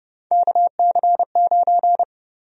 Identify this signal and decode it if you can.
Morse code, heard as a keyed tone, about 31 wpm, 700 Hz. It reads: KC9
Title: USA Call Sign Prefixes - 30wpm